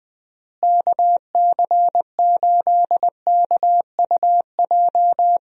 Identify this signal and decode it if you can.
Morse code, heard as a keyed tone, about 20 wpm, 700 Hz. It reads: KC8KUJ
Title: USA Call Signs - 6 Characters - Repeated 3x - 20wpm